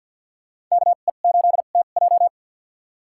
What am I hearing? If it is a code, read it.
Morse code, heard as a keyed tone, about 53 wpm, spaced as for 37 wpm, 700 Hz. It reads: KE8TJ
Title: USA Call Signs - 5 Characters - 2x Character Spacing - 50wpm